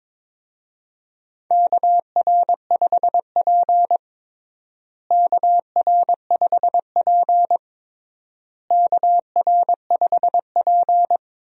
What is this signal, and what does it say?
Morse code, heard as a keyed tone, about 22 wpm, 700 Hz. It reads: KR5P KR5P KR5P
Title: USA Call Signs - 4 Characters - Repeated 3x - 22wpm